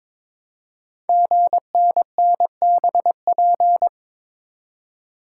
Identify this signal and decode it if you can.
Morse code, heard as a keyed tone, about 22 wpm, 700 Hz. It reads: GNNBP
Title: Code Groups - 5 Letters - 22wpm